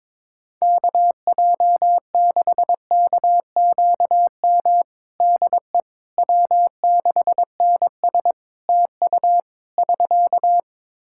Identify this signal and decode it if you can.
Morse code, heard as a keyed tone, about 21 wpm, 700 Hz. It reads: KJ6KQM DE W6NS TU <SK>
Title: Practice QSO - Exchange 4 - Ending - 22wpm